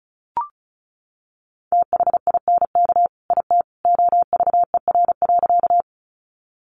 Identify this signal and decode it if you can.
Morse code, heard as a keyed tone, about 34 wpm, 700 Hz. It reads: THINK IT OVER.
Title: Sentences from Top 200 Words 35wpm